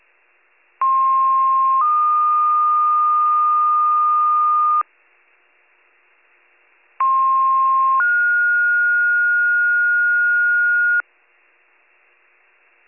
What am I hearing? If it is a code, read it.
Two-tone page, heard as a one-second tone, then a three-second tone, about 1060 then 1220 Hz, about 1060 then 1430 Hz. It is heard from 0.8 to 11.0 s.